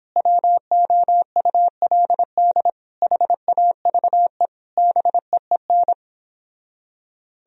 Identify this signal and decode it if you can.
Morse code, heard as a keyed tone, about 31 wpm, 700 Hz. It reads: WOULD HAVE BEEN